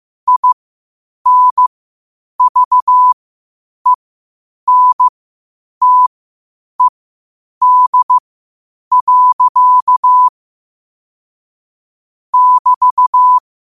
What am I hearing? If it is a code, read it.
Morse code, heard as a keyed tone, about 15 wpm, spaced as for 8 wpm, 1000 Hz. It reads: INVENTED. <BT>
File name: qotdfiles / 8wpm